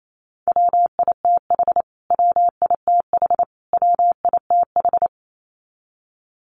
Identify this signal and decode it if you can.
Morse code, heard as a keyed tone, about 28 wpm, 700 Hz. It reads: WITH WITH WITH
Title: Top 200 Words Repeated 28wpm